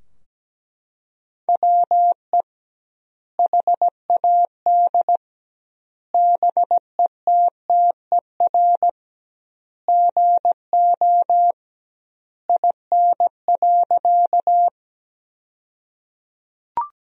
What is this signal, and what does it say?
Morse code, heard as a keyed tone, about 17 wpm, 700 Hz. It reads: WE HAD BETTER GO IN.
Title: 2x Word-Spacing - Sentences from Top 400 Words - 17wpm